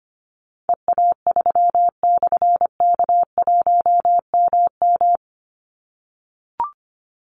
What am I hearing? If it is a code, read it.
Morse code, heard as a keyed tone, about 25 wpm, 700 Hz. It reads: EA3/K1MM
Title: Call Signs - 25wpm